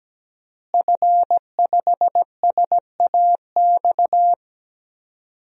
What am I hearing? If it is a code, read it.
Morse code, heard as a keyed tone, about 17 wpm, 700 Hz. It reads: F5SAX
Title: DXCC-100 Call Signs - 5 Characters - Repeated 3x - 17wpm